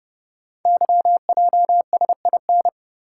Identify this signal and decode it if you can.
Morse code, heard as a keyed tone, about 29 wpm, 700 Hz. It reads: YJSIN
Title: Code Groups - 5 Letters - 30wpm